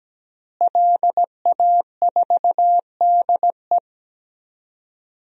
Morse code, heard as a keyed tone, about 17 wpm, 700 Hz. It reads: LA4DE
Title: DXCC-100 Call Signs - 5 Characters - Repeated 2x - 17wpm